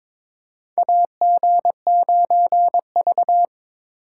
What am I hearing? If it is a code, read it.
Morse code, heard as a keyed tone, about 22 wpm, 700 Hz. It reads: AG9V
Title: USA Call Signs - 4 Characters - 3x Character Spacing - 22wpm